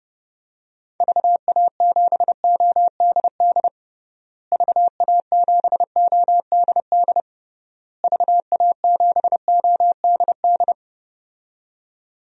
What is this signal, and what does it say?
Morse code, heard as a keyed tone, about 30 wpm, 700 Hz. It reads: VA7ODD VA7ODD VA7ODD
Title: DXCC-100 Call Signs - 6 Characters - Repeated 3x - 30wpm